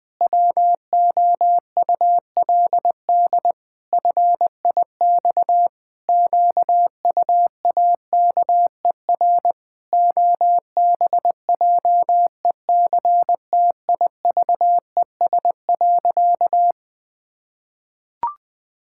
Morse code, heard as a keyed tone, about 20 wpm, 700 Hz. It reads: WOULD FIX QUAKER OBJECTIVES.
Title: Pangram Challenge v2 - 20wpm